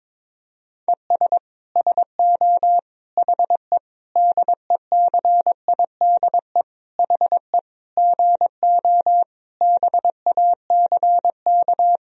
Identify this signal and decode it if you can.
Morse code, heard as a keyed tone, about 22 wpm, 700 Hz. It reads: ES SO HE DECIDE HE GO BACK
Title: Sentences from Top 500 Words and Abbreviations 22wpm